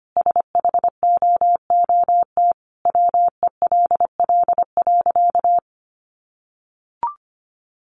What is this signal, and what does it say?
Morse code, heard as a keyed tone, about 25 wpm, 700 Hz. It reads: SHOOT WELL.
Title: Sentences from Top 1200 Words 25wpm